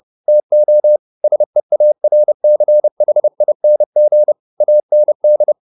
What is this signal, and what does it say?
Morse code, heard as a keyed tone, about 30 wpm, 600 Hz. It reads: TO SEARCHING AND